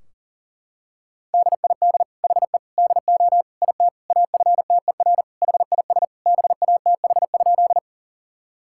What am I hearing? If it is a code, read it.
Morse code, heard as a keyed tone, about 37 wpm, 700 Hz. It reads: DID HE DO IT AFTER HIS BATH?
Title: Sentences from Top 1600 Words 40wpm